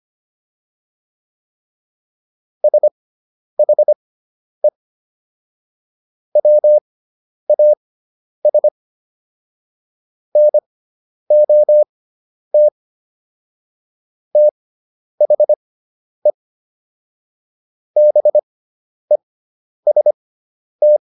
Morse code, heard as a keyed tone, about 25 wpm, spaced as for 10 wpm, 600 Hz. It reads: SHE WAS NOT THE BEST